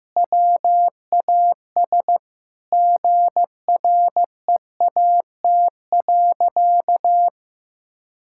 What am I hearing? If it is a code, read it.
Morse code, heard as a keyed tone, about 15 wpm, 700 Hz. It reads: WAS GREAT.
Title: Sentences from Top 1000 Words - Review Part 6 - 15wpm